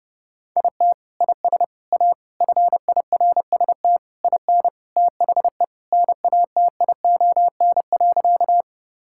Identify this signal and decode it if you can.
Morse code, heard as a keyed tone, about 30 wpm, 700 Hz. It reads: IT IS A FIRST IN THE NATION.
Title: Sentences from Top 1000 Words - Review Part 6 - 30wpm